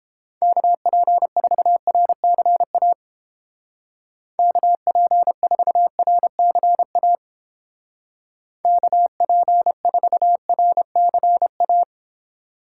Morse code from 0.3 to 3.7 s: KP4RCA

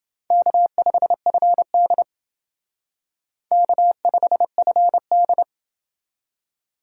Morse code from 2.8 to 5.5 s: K5FD

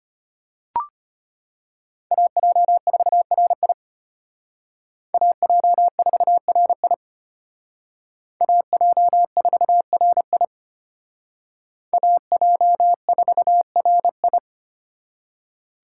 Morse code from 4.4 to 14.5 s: AJ4RI AJ4RI AJ4RI

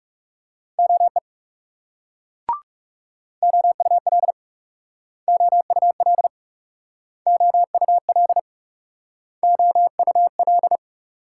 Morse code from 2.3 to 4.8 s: OUL

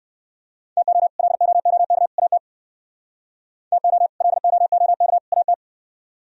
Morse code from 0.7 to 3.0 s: TO LOOK AT

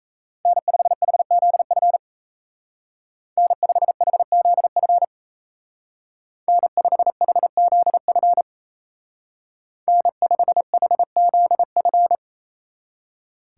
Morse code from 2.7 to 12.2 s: N5HZF N5HZF N5HZF

